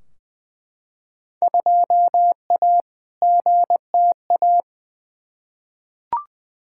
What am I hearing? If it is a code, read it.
Morse code, heard as a keyed tone, about 20 wpm, 700 Hz. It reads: 2A GTA
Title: Field Day Exchanges Encore - 20wpm